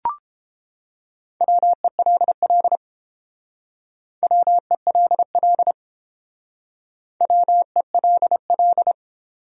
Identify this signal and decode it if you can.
Morse code, heard as a keyed tone, about 28 wpm, 700 Hz. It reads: WELL WELL WELL